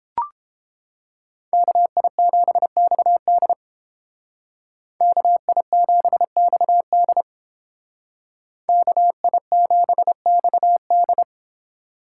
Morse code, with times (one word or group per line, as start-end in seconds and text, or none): 1.5-3.5 KI7XD
5.0-7.2 KI7XD
8.7-11.2 KI7XD